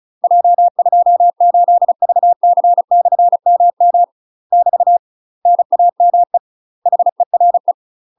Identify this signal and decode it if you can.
Morse code, heard as a keyed tone, about 34 wpm, 700 Hz. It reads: J28VC/MM <BT> NAME HERE
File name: UK MASTER QSOs TRAINING/35WPM